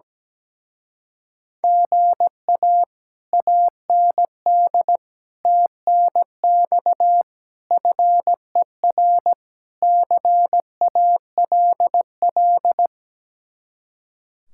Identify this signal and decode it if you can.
Morse code, heard as a keyed tone, about 17 wpm, 700 Hz. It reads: GA AND TNX FER CALL
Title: Speed-Racing Common QSO Phrases - 17wpm